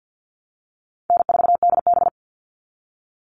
Morse code, heard as a keyed tone, about 50 wpm, 700 Hz. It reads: N4DB
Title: All Cut Numbers - 4-Digit 50wpm